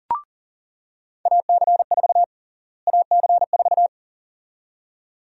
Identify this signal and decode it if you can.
Morse code, heard as a keyed tone, about 40 wpm, 700 Hz. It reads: AC4 AC4
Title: USA Call Sign Prefixes - Repeated 2x - 40wpm